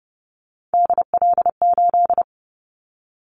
Morse code, heard as a keyed tone, about 30 wpm, 700 Hz. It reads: DL8
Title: Callsign Prefixes for Top 50 DXCC - 30wpm